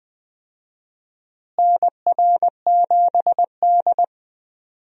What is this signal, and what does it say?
Morse code, heard as a keyed tone, about 21 wpm, 700 Hz. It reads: NR7D
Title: USA Call Signs - 4 Characters - Repeated 3x - 20wpm